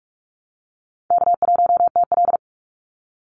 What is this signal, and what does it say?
Morse code, heard as a keyed tone, about 44 wpm, 700 Hz. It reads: K1TL
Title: Easy Call Signs Repeated 2x - 45wpm